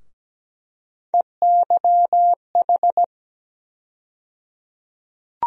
Morse code, heard as a keyed tone, about 17 wpm, 700 Hz. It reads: EYH
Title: USA Call Sign Postfixes - 3 Letters 17wpm